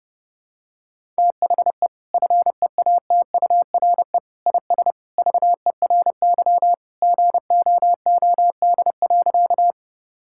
Morse code, heard as a keyed tone, about 32 wpm, 700 Hz. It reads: THE FEATURE IS VERY GOOD.
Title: Sentences from Top 1000 Words - Review Part 5 - 30wpm